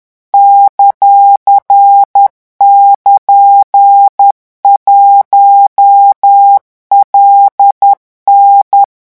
Morse code, heard as a keyed tone, about 10 wpm, 800 Hz. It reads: ;<KN>1LN
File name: morse mp3 / revision16-vitesse 10 mots